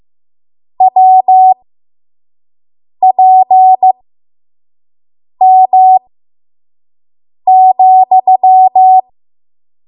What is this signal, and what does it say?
Morse code, heard as a keyed tone, about 15 wpm, spaced as for 5 wpm, 750 Hz. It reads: WPM,
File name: Morse / 05WPM